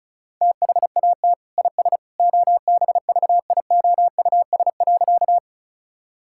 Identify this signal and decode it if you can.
Morse code, heard as a keyed tone, about 34 wpm, 700 Hz. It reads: THAT IS OBVIOUS.
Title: Sentences from Top 1600 Words 35wpm